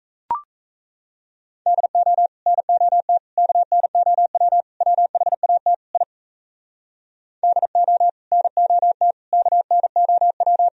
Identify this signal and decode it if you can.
Morse code, heard as a keyed tone, about 38 wpm, 700 Hz. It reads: DO NOT KNOW WHAT I DO NOT KNOW